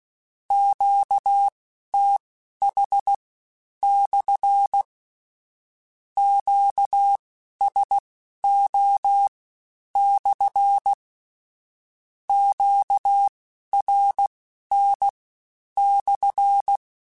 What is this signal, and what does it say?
Morse code, heard as a keyed tone, about 16 wpm, spaced as for 10 wpm, 800 Hz. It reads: QTH/ QSO/ QRN/
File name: Q Calls / Q Calls-mp3s / Q Calls SPSL1 @15